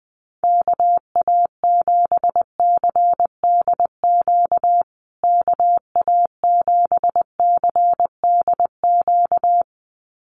Morse code, heard as a keyed tone, about 20 wpm, 700 Hz. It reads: KA7CDQ KA7CDQ
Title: USA Call Signs Repeated 2x 20wpm